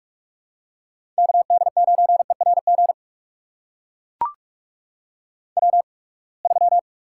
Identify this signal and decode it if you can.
Morse code, heard as a keyed tone, about 44 wpm, spaced as for 31 wpm, 700 Hz. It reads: KD9ERG W3
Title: USA Call Signs - 8x Character Spacing - 45wpm